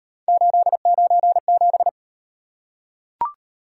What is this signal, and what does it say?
Morse code, heard as a keyed tone, about 36 wpm, 700 Hz. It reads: 897